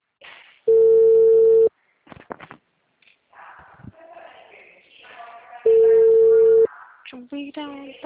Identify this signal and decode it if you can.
Ringback tone (China, 450 Hz), with speech.